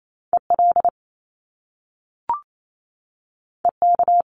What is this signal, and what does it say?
Morse code, heard as a keyed tone, about 28 wpm, 700 Hz. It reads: EL EK
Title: Top Two Letter Combinations in Words 28wpm